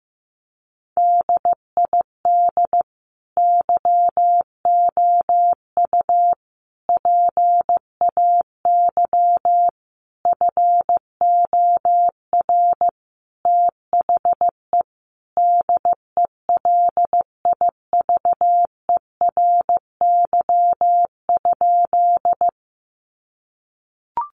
Morse code, heard as a keyed tone, about 15 wpm, 700 Hz. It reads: DID YOU PAY FOR THE DELIVERY?
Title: Sentences from Top 1500 Words 15wpm